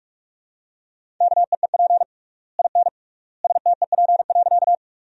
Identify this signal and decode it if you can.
Morse code, heard as a keyed tone, about 46 wpm, 700 Hz. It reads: KEEP IN STEP.
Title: 3x Word-Spacing - Sentences from Top 400 Words - 45wpm